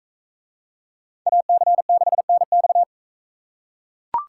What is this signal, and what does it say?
Morse code, heard as a keyed tone, about 41 wpm, 700 Hz. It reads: AC6NX